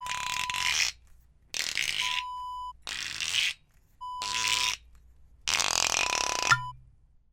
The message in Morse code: none